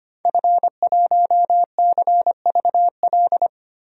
Morse code, heard as a keyed tone, about 25 wpm, 700 Hz. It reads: F1CVL
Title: DXCC-100 Call Signs - 5 Characters - 25wpm